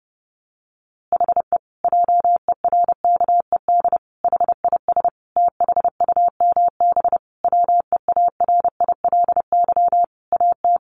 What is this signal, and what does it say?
Morse code, heard as a keyed tone, about 30 wpm, 700 Hz. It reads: HE JERKED HIS THUMB WEARILY AT
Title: A Sound Of Thunder 30wpm